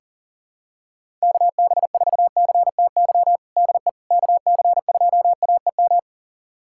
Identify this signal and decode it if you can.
Morse code, heard as a keyed tone, about 40 wpm, 700 Hz. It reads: KB4CTY DE KC2AEM
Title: USA Callsign DE Callsign 40wpm